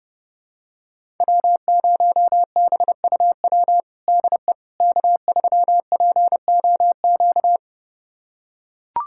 Morse code, heard as a keyed tone, about 32 wpm, 700 Hz. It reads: W0BUW DE K3POQ
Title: USA Callsign DE Callsign - 5 Characters - 30wpm